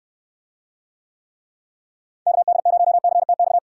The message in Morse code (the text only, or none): KD0CTX